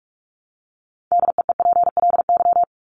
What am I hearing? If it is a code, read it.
Morse code, heard as a keyed tone, about 44 wpm, 700 Hz. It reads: DEEPLY
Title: Top 1751-2000 Words 45wpm